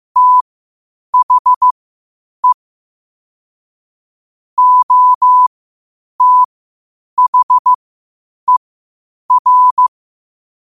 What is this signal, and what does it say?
Morse code, heard as a keyed tone, about 15 wpm, spaced as for 8 wpm, 1000 Hz. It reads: THE OTHER
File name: qotdfiles / 8wpm